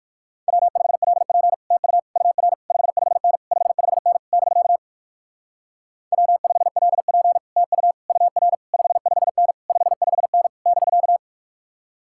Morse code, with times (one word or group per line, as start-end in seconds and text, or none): 0.5-1.5 W5LP
1.7-2.0 TU
2.2-2.5 UR
2.7-3.4 55N
3.5-4.2 55N
4.3-4.8 <BK>
6.1-7.4 W5LP
7.6-7.9 TU
8.1-8.5 UR
8.7-9.5 55N
9.7-10.5 55N
10.7-11.2 <BK>